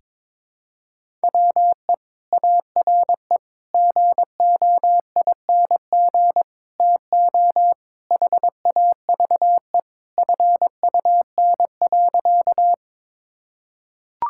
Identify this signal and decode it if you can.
Morse code, heard as a keyed tone, about 22 wpm, 700 Hz. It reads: WE ARE GOING TO HAVE FUN.
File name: practice / 22wpm / Sentences with 4-Word Phrases - 22wpm